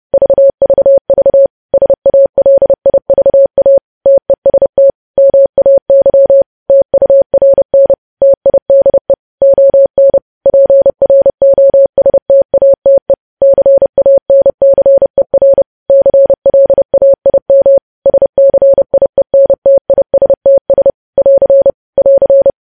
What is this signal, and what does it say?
Morse code, heard as a keyed tone, about 30 wpm, 550 Hz. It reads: VVV SALIVA TEST MAY TURN TIDE ON PROSTATE CANCER CLAIM SCIENTISTS <AR> <AR>
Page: news headlines in morse code at 30 wpm